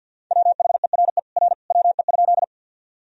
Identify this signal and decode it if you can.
Morse code, heard as a keyed tone, about 52 wpm, 700 Hz. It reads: WHERE R WE?
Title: Sentences from Top 200 Words and Abbreviations 50wpm